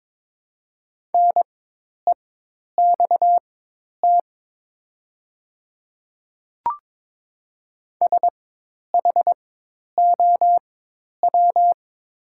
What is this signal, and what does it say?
Morse code, heard as a keyed tone, about 22 wpm, spaced as for 10 wpm, 700 Hz. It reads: NEXT SHOW
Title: 4x Character Spacing - Top 300 Words - 22wpm